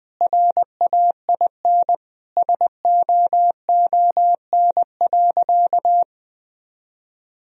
Morse code, heard as a keyed tone, about 20 wpm, 700 Hz. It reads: RAIN SOON.
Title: Sentences from Top 1000 Words - Review Part 1 - 20wpm